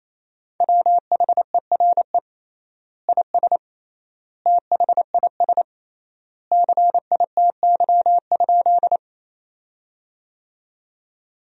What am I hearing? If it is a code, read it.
Morse code, heard as a keyed tone, about 28 wpm, 700 Hz. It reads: WHERE IS THIS CITY?
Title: 3x Word-Spacing - Sentences from Top 300 Words - 28wpm